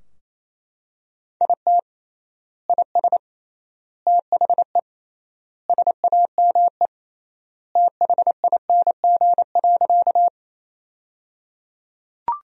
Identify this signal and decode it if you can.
Morse code, heard as a keyed tone, about 28 wpm, 700 Hz. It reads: IT IS THE SAME THING.
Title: 3x Word-Spacing - Sentences from Top 200 Words - 28wpm